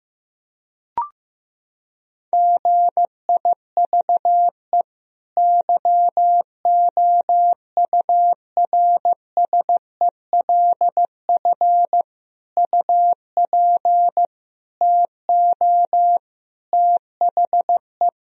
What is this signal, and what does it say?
Morse code, heard as a keyed tone, about 15 wpm, 700 Hz. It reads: GIVE YOURSELF UP TO THE
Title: Sentences from Top 1000 Words - Review Part 4 - 15wpm